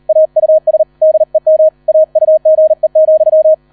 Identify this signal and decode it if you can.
Morse code, heard as a keyed tone, about 38 wpm, 630 Hz. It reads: AUS DEM AUGE,